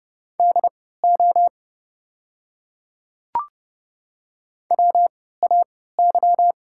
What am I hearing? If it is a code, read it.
Morse code, heard as a keyed tone, about 32 wpm, spaced as for 17 wpm, 690 Hz. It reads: DO WAY
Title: 3x Character Spacing - Top 200 Words - 30wpm